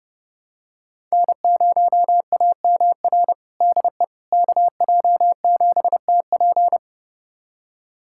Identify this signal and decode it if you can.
Morse code, heard as a keyed tone, about 30 wpm, 700 Hz. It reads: N0AMR DE KJ7TP
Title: USA Callsign DE Callsign - 5 Characters - 30wpm